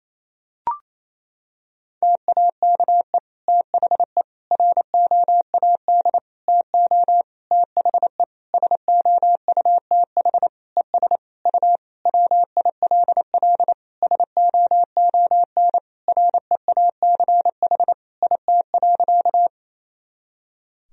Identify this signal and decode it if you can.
Morse code, heard as a keyed tone, about 27 wpm, 700 Hz. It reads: TAKE THE ROAD TO THE SOUTH ES U WILL SOON REACH IT.
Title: Sentences from Top 400 Words and Abbreviations 28wpm